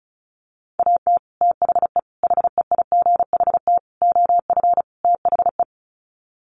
Morse code, heard as a keyed tone, about 34 wpm, 700 Hz. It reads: AT THE HEIGHT OF THE
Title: Sets of 5 Words 35wpm